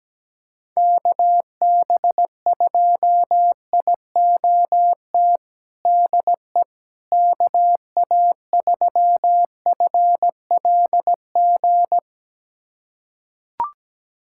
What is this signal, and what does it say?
Morse code, heard as a keyed tone, about 17 wpm, 700 Hz. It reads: KB2IOT DE KA3FLG
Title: USA Callsign DE Callsign - 6 Characters - 17wpm